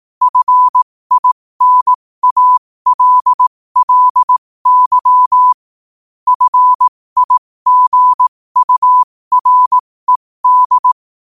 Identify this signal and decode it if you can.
Morse code, heard as a keyed tone, about 18 wpm, spaced as for 15 wpm, 1000 Hz. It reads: FINALLY FIGURED